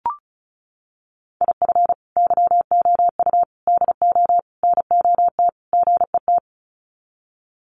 Morse code, heard as a keyed tone, about 34 wpm, 700 Hz. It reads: IF YOU DO NOT GET
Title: Sets of 5 Words Encore2 35wpm